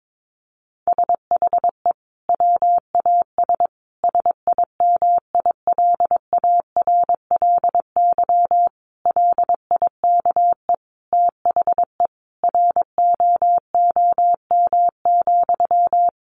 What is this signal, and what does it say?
Morse code, heard as a keyed tone, about 22 wpm, 700 Hz. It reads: SHE WAS SIMILARLY LIKE THE ROOM,
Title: Sentences from Top 1400 Words 22wpm